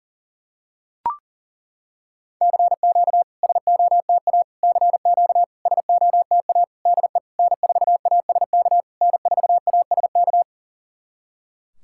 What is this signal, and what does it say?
Morse code, heard as a keyed tone, about 40 wpm, 690 Hz. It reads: CQ SOTA CQ SOTA DE N4ASK N4ASK
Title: SOTA QSO Practice 40wpm